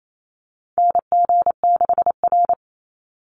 Morse code, heard as a keyed tone, about 28 wpm, 700 Hz. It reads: NG6R
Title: NA SOTA Chaser Callsigns 28wpm